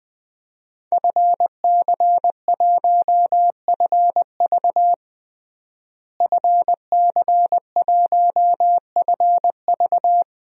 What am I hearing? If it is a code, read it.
Morse code, heard as a keyed tone, about 20 wpm, 700 Hz. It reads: FC1FV FC1FV
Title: DXCC-100 Call Signs - 5 Characters - Repeated 3x - 20wpm